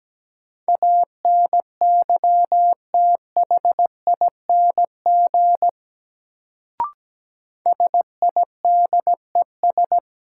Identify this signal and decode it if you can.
Morse code, heard as a keyed tone, about 17 wpm, 700 Hz. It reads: ANYTHING SIDES
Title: Sets of 4 Words - New Word Drill - 17wpm